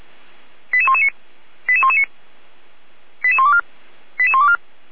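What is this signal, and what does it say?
Five-tone selective call (ZVEI), heard as quick runs of five tones, tones of about 70 ms. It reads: 88119 88119 88124 88124
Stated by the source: ZVEI five-tone selective call received by SDR, from the feuerwehr-orlamuende.org fire brigade audio server.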